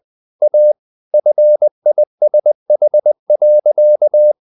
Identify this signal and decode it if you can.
Morse code, heard as a keyed tone, about 20 wpm, 600 Hz. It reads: A FISH.
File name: the old man and the sea part 2 20wpm